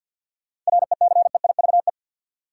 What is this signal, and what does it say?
Morse code, heard as a keyed tone, about 49 wpm, 700 Hz. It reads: RECEIVE
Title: Sentences from Top 1000 Words - New Word Drill - 50wpm